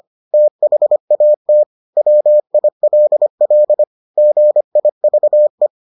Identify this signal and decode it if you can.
Morse code, heard as a keyed tone, about 25 wpm, 600 Hz. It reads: THAT WILL GIVE